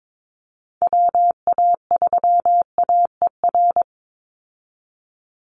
Morse code, heard as a keyed tone, about 22 wpm, 700 Hz. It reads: WA3AER
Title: Call Signs Repeated 2x - 22wpm